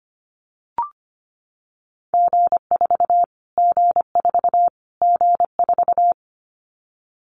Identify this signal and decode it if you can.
Morse code, heard as a keyed tone, about 25 wpm, 700 Hz. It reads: G4 G4 G4
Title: Callsign Prefixes for Top 25 DXCC - Repeated 3x - 25wpm